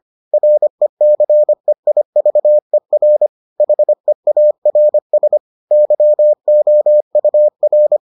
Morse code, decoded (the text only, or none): RECEIVER HEARS YOUR